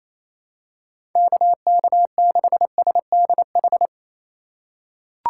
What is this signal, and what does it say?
Morse code, heard as a keyed tone, about 28 wpm, 700 Hz. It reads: KK6SDH
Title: USA Call Signs - 4x Character Spacing - 28wpm